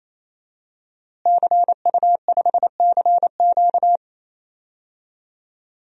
Morse code, heard as a keyed tone, about 28 wpm, 700 Hz. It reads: CU5CQ
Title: DXCC-100 Call Signs - 5 Characters - Repeated 3x - 28wpm